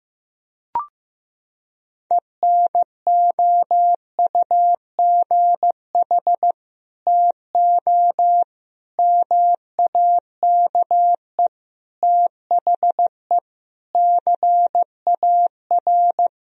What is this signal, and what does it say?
Morse code, heard as a keyed tone, about 15 wpm, 700 Hz. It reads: ENOUGH TO MAKE THE CAR
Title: Sentences from Top 1000 Words - Review Part 1 - 15wpm